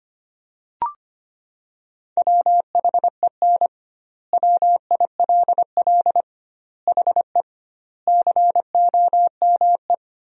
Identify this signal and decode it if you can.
Morse code, heard as a keyed tone, about 25 wpm, 700 Hz. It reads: WHEN WILL HE COME
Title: 2x Word-Spacing - Sentences from Top 200 Words - 25wpm